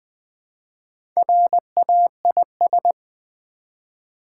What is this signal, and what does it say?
Morse code, heard as a keyed tone, about 20 wpm, 700 Hz. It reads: RAIS
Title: Build A Word - 1 Letter At A Time - 5-Letter Words - 20wpm